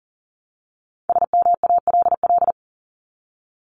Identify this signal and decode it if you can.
Morse code, heard as a keyed tone, about 40 wpm, 700 Hz. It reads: SMALL
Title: Top 200 Words 40wpm